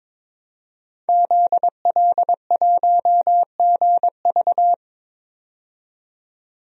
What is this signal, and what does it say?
Morse code, heard as a keyed tone, about 22 wpm, 700 Hz. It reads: ZL1GV
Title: DXCC-100 Call Signs - 5 Characters - 22wpm